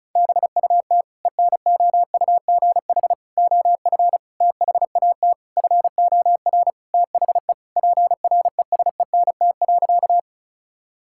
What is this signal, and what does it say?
Morse code, heard as a keyed tone, about 36 wpm, 700 Hz. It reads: BUT ENOUGH OF THAT FOR THE PRESENT.
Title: Sentences from Top 1000 Words - Review Part 5 - 35wpm